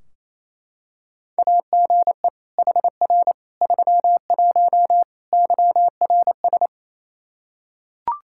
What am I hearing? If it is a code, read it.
Morse code, heard as a keyed tone, about 28 wpm, 700 Hz. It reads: AGE HR 31 YRS
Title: Practice QSO - Exchange 3 - More Info - 28wpm